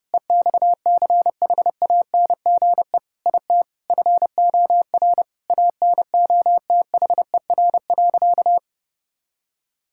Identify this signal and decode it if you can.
Morse code, heard as a keyed tone, about 32 wpm, 700 Hz. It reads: EXCHANGE IT FOR ANOTHER.
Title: Sentences from Top 1000 Words - Review Part 6 - 30wpm